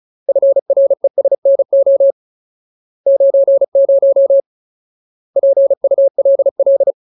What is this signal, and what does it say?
Morse code, heard as a keyed tone, about 34 wpm, 550 Hz. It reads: FRESNO 90 PULL
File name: Ultimate Word List / 35WPM Ultimate